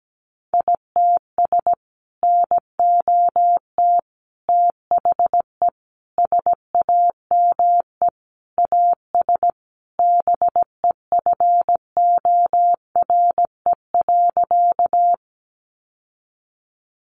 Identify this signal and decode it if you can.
Morse code, heard as a keyed tone, about 17 wpm, 700 Hz. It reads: ITS NOT THE SAME AS BEFORE.
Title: Sentences from Top 600 Words 17wpm